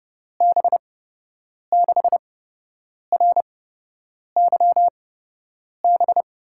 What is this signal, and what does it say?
Morse code, heard as a keyed tone, about 31 wpm, spaced as for 8 wpm, 700 Hz. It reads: B6RYB
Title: USA Call Signs - 6 Characters - 8x Character Spacing - 30wpm